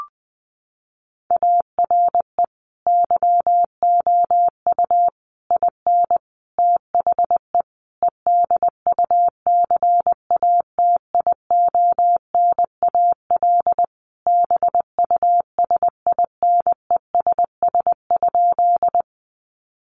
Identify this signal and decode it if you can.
Morse code, heard as a keyed tone, about 20 wpm, 700 Hz. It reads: ARE YOU IN THE EDUCATIONAL BUSINESS?
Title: Sentences from Top 1200 Words 20wpm